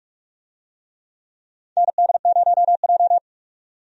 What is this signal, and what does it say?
Morse code, heard as a keyed tone, about 44 wpm, 700 Hz. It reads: ND0J
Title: USA Call Signs - 4 Characters - 45wpm